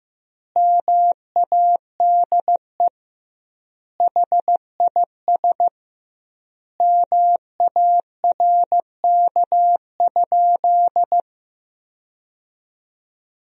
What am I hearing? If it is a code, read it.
Morse code, heard as a keyed tone, about 15 wpm, 700 Hz. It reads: MADE HIS MARK?
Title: 2x Word-Spacing - Sentences from Top 300 Words - 15wpm